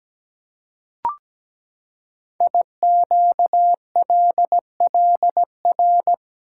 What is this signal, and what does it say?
Morse code, heard as a keyed tone, about 17 wpm, 700 Hz. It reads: IQLLR
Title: Code Groups - 5 Letters - 17wpm